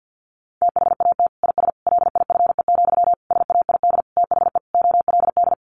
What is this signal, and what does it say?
Morse code, heard as a keyed tone, about 50 wpm, 700 Hz. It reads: THAT IS LIFE, SAID THE OLD
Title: Sentences from Top 900 Words 50wpm